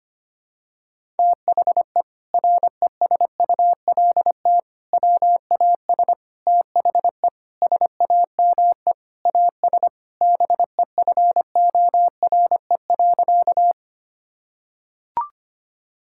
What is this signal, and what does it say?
Morse code, heard as a keyed tone, about 25 wpm, 700 Hz. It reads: THE RESULT WAS THE SAME AS BEFORE.
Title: Sentences from Top 1000 Words - Review Part 5 - 25wpm